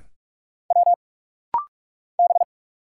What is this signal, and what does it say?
Morse code, heard as a keyed tone, about 45 wpm, 700 Hz. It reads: W B
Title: Single Letter-Number Alt - Rapid-Fire 45wpm